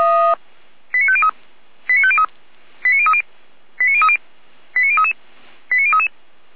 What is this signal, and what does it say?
Five-tone selective call (ZVEI), heard as quick runs of five tones, tones of about 70 ms. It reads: 89503 89503 89030 89030 89033 89033
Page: SDR/ZVEI